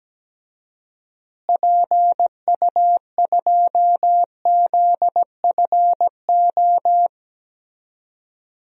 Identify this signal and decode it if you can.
Morse code, heard as a keyed tone, about 17 wpm, 700 Hz. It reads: PU2ZFO
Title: DXCC-100 Call Signs - 6 Characters - Repeated 2x - 17wpm